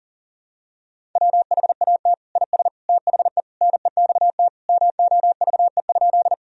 Morse code, heard as a keyed tone, about 37 wpm, 700 Hz. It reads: WHAT IS THE NEXT MOVE?
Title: Sentences from Top 1000 Words - Review Part 1 - 40wpm